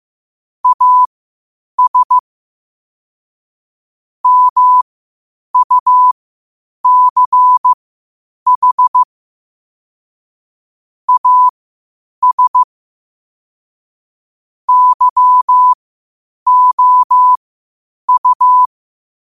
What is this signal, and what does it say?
Morse code, heard as a keyed tone, about 15 wpm, spaced as for 8 wpm, 1000 Hz. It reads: AS MUCH AS YOU
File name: qotdfiles / 8wpm